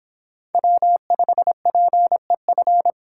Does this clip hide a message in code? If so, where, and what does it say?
Yes: Morse code from 0.5 to 2.9 s, heard as a keyed tone, about 30 wpm, 700 Hz. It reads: W5PEF